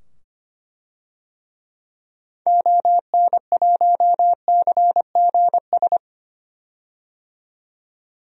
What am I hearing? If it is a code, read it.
Morse code, heard as a keyed tone, about 25 wpm, 700 Hz. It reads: ON1CGS
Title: DXCC-100 Call Signs - 6 Characters - 25wpm